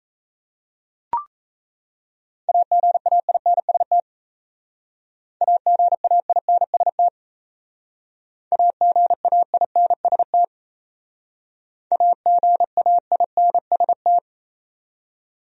Morse code, heard as a keyed tone, about 36 wpm, 700 Hz. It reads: AGAINST AGAINST AGAINST AGAINST